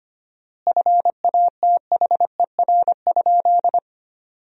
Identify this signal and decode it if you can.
Morse code, heard as a keyed tone, about 25 wpm, 700 Hz. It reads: FATHER?
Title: 4x Word-Spacing - Sentences from Top 400 Words - 25wpm